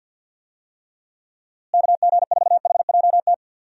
Morse code, heard as a keyed tone, about 52 wpm, 700 Hz. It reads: KG4HJT
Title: USA Call Signs - 6 Characters - Repeated 3x - 50wpm